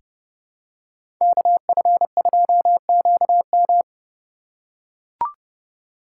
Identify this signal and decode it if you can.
Morse code, heard as a keyed tone, about 30 wpm, 690 Hz. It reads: KF2QM
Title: USA Call Signs - 3x Character Spacing - 30wpm